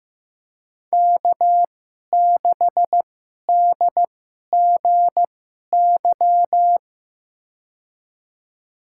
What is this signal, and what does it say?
Morse code, heard as a keyed tone, about 15 wpm, spaced as for 11 wpm, 700 Hz. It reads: K6DGY
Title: USA Call Signs - 5 Characters - 2x Character Spacing - 15wpm